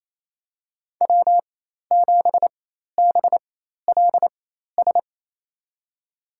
Morse code, heard as a keyed tone, about 28 wpm, spaced as for 13 wpm, 700 Hz. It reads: W7BLS
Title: USA Call Signs - 5 Characters - 4x Character Spacing - 28wpm